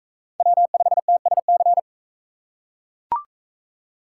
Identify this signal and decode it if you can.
Morse code, heard as a keyed tone, about 40 wpm, 700 Hz. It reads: W5TSC